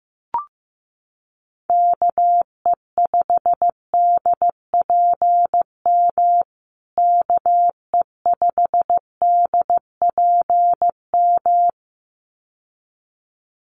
Morse code, heard as a keyed tone, about 15 wpm, 700 Hz. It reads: KE5DPM KE5DPM
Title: USA Call Signs Repeated 2x 15wpm